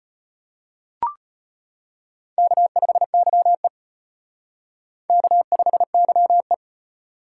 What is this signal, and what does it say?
Morse code, heard as a keyed tone, about 33 wpm, 700 Hz. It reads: K5YE K5YE